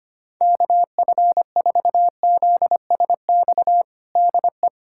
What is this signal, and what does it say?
Morse code, heard as a keyed tone, about 25 wpm, 700 Hz. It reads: KF4ZSX DE
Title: USA Callsign DE Callsign 25wpm